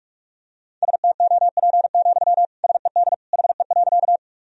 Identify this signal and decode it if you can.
Morse code, heard as a keyed tone, about 46 wpm, 700 Hz. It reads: STOP, SED HE.
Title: Sentences from Top 300 Words and Abbreviations 45wpm